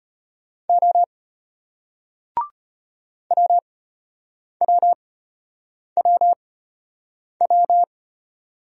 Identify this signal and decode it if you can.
Morse code, heard as a keyed tone, about 34 wpm, spaced as for 8 wpm, 700 Hz. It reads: O WWWW